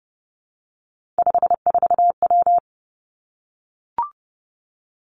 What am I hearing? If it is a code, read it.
Morse code, heard as a keyed tone, about 30 wpm, 700 Hz. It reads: 54W
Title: Three Challenging Letter-Number Combinations 30wpm